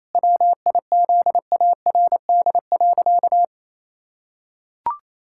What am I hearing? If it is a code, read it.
Morse code, heard as a keyed tone, about 28 wpm, 700 Hz. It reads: WIZARD.
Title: Pangram Challenge v2 - 28wpm